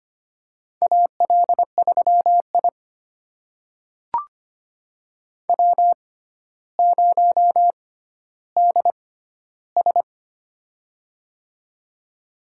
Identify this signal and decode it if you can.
Morse code, heard as a keyed tone, about 25 wpm, spaced as for 13 wpm, 700 Hz. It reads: AL3I W0DS
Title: USA Call Signs - 6x Character Spacing - 25wpm